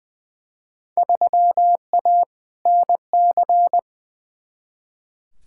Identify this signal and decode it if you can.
Morse code, heard as a keyed tone, about 20 wpm, 700 Hz. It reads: 3A NC
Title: Field Day Exchanges - 20wpm